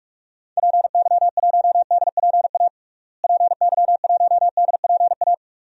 Morse code, heard as a keyed tone, about 46 wpm, 700 Hz. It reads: PY1DPA PY1DPA
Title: DXCC-100 Call Signs - 6 Characters - Repeated 2x - 45wpm